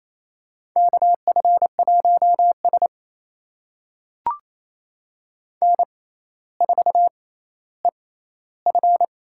Morse code, heard as a keyed tone, about 28 wpm, spaced as for 14 wpm, 700 Hz. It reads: KF1S N4EF
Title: USA Call Signs - 4 Characters - 6x Character Spacing - 28wpm